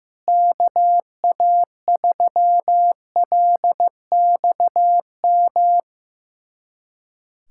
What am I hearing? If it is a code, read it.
Morse code, heard as a keyed tone, about 15 wpm, 700 Hz. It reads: KA3LXM
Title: USA Callsign DE Callsign - 6 Characters - 15wpm